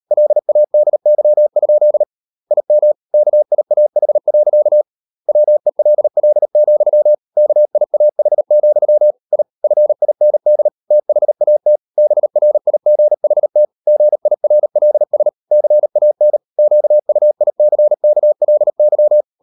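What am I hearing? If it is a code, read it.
Morse code, heard as a keyed tone, about 38 wpm, 600 Hz. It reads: LADY? IM KIAH. WELL, KIAH, I FIND THAT BRIGHT GIRLS CAN QUICKLY
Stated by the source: Morse code practice at 38 WPM, sending Ken Follett's novel Never in English, from CW med Gnister.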